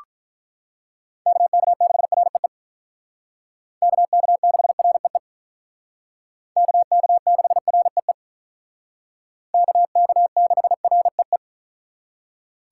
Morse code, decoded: KK6REE KK6REE KK6REE KK6REE

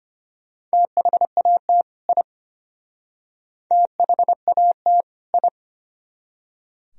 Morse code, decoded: THAT I THAT I

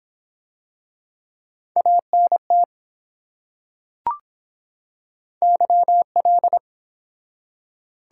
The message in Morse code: ANT YL

